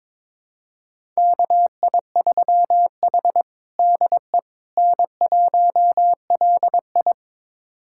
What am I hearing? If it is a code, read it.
Morse code, heard as a keyed tone, about 22 wpm, 700 Hz. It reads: KI3H DE N1LI
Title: USA Callsign DE Callsign - 4 Characters - 22wpm